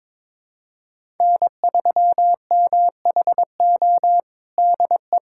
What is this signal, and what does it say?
Morse code, heard as a keyed tone, about 22 wpm, 700 Hz. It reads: N3MHO DE
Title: USA Callsign DE Callsign - 5 Characters - 22wpm